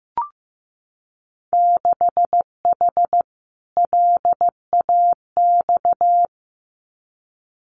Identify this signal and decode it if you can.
Morse code, heard as a keyed tone, about 15 wpm, 700 Hz. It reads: 6H LAX
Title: Winter Field Day Exchanges Encore 15wpm